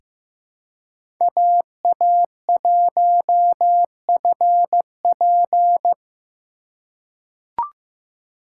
Morse code, heard as a keyed tone, about 15 wpm, 700 Hz. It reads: AA1FP
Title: USA Call Signs - 5 Characters - 3x Character Spacing - 15wpm